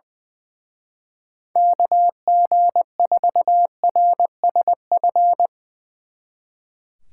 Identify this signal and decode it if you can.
Morse code, heard as a keyed tone, about 20 wpm, 700 Hz. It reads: KG4RSF
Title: Speed Racing - USA 2x3 Call Signs 20wpm